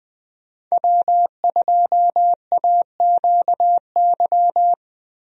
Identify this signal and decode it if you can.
Morse code, heard as a keyed tone, about 20 wpm, 700 Hz. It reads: W2AQY
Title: USA Call Signs - 5 Characters - Repeated 3x - 20wpm